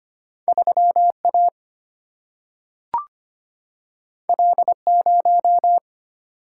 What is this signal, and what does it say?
Morse code, heard as a keyed tone, about 25 wpm, 700 Hz. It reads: 3A L0
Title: Single-Letter Single-Number 25wpm